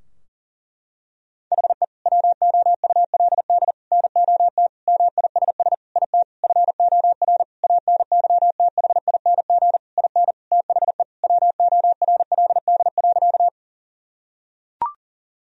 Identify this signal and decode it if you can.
Morse code, heard as a keyed tone, about 37 wpm, 700 Hz. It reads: HE WOULD NOT MISS IT FOR ANYTHING IN THE WORLD.
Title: Sentences from Top 1000 Words - Review Part 3 - 40wpm